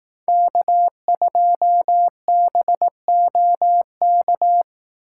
Morse code, heard as a keyed tone, about 18 wpm, 700 Hz. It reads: K2BOK
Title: Speed Racing - USA 1x3 Call Signs 15wpm